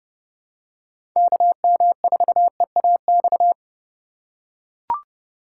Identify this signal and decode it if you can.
Morse code, heard as a keyed tone, about 30 wpm, 700 Hz. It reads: KM4EAX
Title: USA Call Signs - 6 Characters - 4x Character Spacing - 30wpm